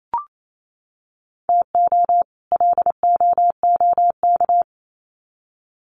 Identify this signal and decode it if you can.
Morse code, heard as a keyed tone, about 28 wpm, 700 Hz. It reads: TO LOOK
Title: Sets of 2 Words 28wpm